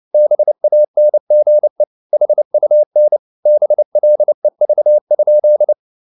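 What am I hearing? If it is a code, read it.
Morse code, heard as a keyed tone, about 29 wpm, 600 Hz. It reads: BANGE HUN BLEV?
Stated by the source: Fast (QRQ) Morse at 29 wpm, from CW med Gnister.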